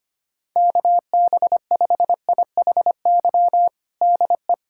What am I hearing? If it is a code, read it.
Morse code, heard as a keyed tone, about 25 wpm, 700 Hz. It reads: KB5IHY DE
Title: USA Callsign DE Callsign - 6 Characters - 25wpm